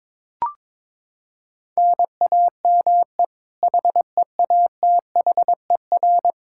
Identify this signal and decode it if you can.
Morse code, heard as a keyed tone, about 22 wpm, 700 Hz. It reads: NAME HEATHER
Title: K1USN SST QSO Practice 22wpm